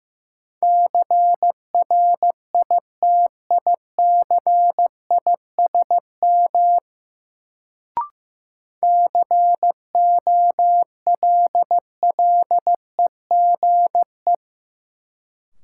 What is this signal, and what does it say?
Morse code, heard as a keyed tone, about 15 wpm, 700 Hz. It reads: CRITICISM COLLEGE
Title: Sentences from Top 1300 Words - New Word Drill Alt - 15wpm